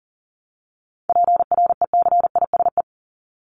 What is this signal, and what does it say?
Morse code, heard as a keyed tone, about 40 wpm, 700 Hz. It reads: PRECISE
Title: LY Suffix - Top 100 Words with Base Words 40wpm